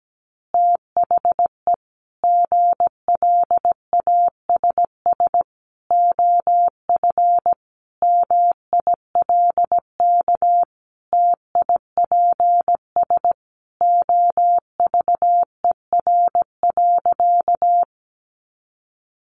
Morse code, heard as a keyed tone, about 17 wpm, 700 Hz. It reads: THE GLASS OF MILK TIPS OVER.
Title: Additional Easy Phrases 17wpm